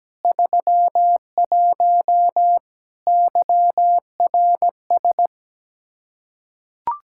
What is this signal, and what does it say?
Morse code, heard as a keyed tone, about 17 wpm, 700 Hz. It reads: 31 YRS
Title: Practice QSO - Exchange 3 - More Info - 17wpm